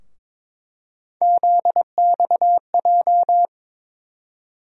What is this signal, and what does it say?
Morse code, heard as a keyed tone, about 21 wpm, 700 Hz. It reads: ZXJ
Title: USA Call Sign Postfixes - 3 Letters 22wpm